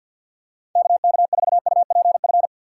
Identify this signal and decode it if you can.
Morse code, heard as a keyed tone, about 50 wpm, 700 Hz. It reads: K K 4 U P F
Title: USA Call Signs - 6x Character Spacing - 50wpm